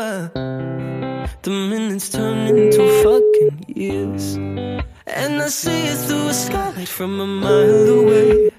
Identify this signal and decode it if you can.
Ringback tone (Europe, 425 Hz), with noise.